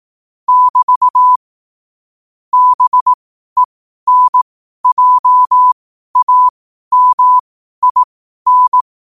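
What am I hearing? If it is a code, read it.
Morse code, heard as a keyed tone, about 18 wpm, spaced as for 12 wpm, 1000 Hz. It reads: <BT> BENJAMIN
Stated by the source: Morse code sent at 12 words per minute.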